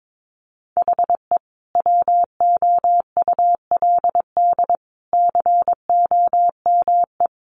Morse code, heard as a keyed tone, about 22 wpm, 700 Hz. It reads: HE WOULD COME
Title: Sentences from Top 1300 Words 22wpm